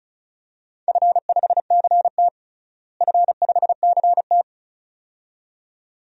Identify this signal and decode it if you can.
Morse code, heard as a keyed tone, about 34 wpm, 700 Hz. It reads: F5CT F5CT
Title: DXCC-100 Call Signs - 4 Characters - Repeated 2x - 35wpm